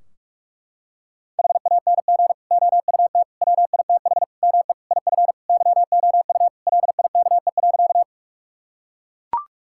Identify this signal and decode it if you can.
Morse code, heard as a keyed tone, about 41 wpm, 700 Hz. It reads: HANG OUT WITH ME IF YOU LIKE.
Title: Sentences from Top 1600 Words 45wpm